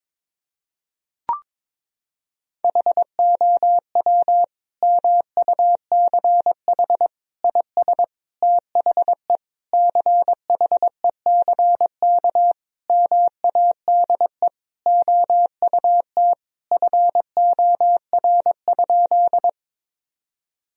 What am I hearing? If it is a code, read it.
Morse code, heard as a keyed tone, about 21 wpm, 700 Hz. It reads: HOW MUCH IS THE CHECK MADE OUT FOR?
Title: Sentences from Top 1000 Words - Review Part 3 - 22wpm